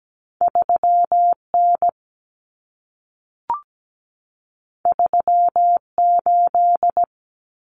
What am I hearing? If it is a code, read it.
Morse code, heard as a keyed tone, about 17 wpm, 700 Hz. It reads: 3N 38
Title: Common Cut Numbers - 2-Digit 17wpm